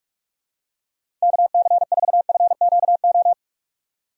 Morse code, heard as a keyed tone, about 46 wpm, 700 Hz. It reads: KC4FQO
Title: USA Call Signs - 6 Characters - 45wpm